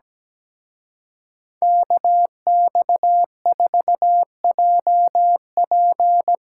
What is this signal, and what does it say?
Morse code, heard as a keyed tone, about 19 wpm, 700 Hz. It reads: KX4JP
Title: Speed Racing - USA Call Signs 17wpm